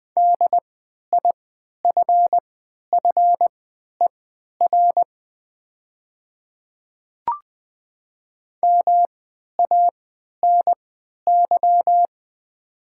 Morse code, heard as a keyed tone, about 20 wpm, spaced as for 11 wpm, 700 Hz. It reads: DIFFER MANY
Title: 3x Character Spacing - Top 300 Words - 20wpm